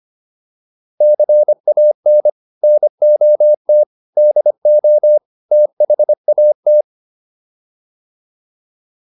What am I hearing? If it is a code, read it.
Morse code, heard as a keyed tone, about 25 wpm, 600 Hz. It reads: CAN NOT DO THAT